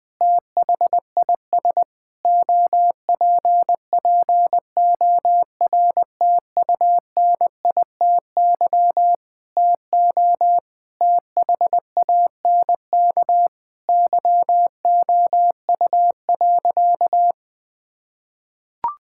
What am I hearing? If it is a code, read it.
Morse code, heard as a keyed tone, about 20 wpm, 700 Hz. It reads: THIS OPPORTUNITY TO THANK YOU.
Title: Sentences with 7-Word Phrases - 20wpm